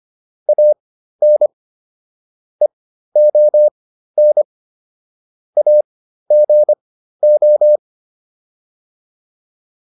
Morse code, heard as a keyed tone, about 25 wpm, spaced as for 13 wpm, 600 Hz. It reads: AN EON AGO
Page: intermediate-practice-files